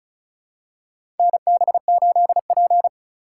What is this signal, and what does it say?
Morse code, heard as a keyed tone, about 34 wpm, 700 Hz. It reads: NB8P
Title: USA Call Signs - 4 Characters - 2x Character Spacing - 35wpm